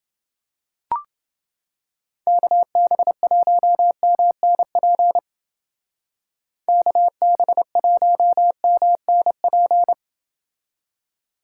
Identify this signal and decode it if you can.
Morse code, heard as a keyed tone, about 28 wpm, 700 Hz. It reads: KB1MNP KB1MNP